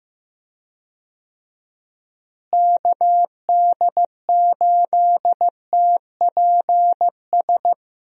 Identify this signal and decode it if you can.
Morse code, heard as a keyed tone, about 15 wpm, 700 Hz. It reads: KD8TPS
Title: USA Call Signs - 6 Characters - Repeated 2x - 15wpm